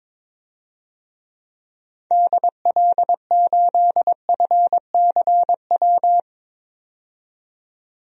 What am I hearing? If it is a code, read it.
Morse code, heard as a keyed tone, about 22 wpm, 700 Hz. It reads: DL8FCW
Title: DXCC-100 Call Signs - 6 Characters - 22wpm